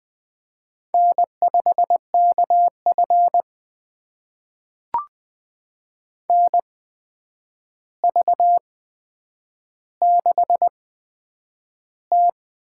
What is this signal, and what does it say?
Morse code, heard as a keyed tone, about 20 wpm, spaced as for 9 wpm, 700 Hz. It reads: N5KF NV6T
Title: USA Call Signs - 4 Characters - 8x Character Spacing - 20wpm